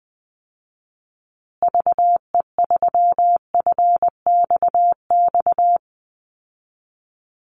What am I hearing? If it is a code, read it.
Morse code, heard as a keyed tone, about 20 wpm, 700 Hz. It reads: VE3FXX
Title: Call Signs Repeated 3x - 20wpm